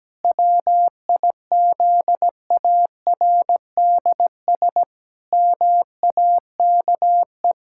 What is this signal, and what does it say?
Morse code, heard as a keyed tone, about 17 wpm, 700 Hz. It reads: WIZARDS MAKE
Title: Pangram Challenge v2 - 17wpm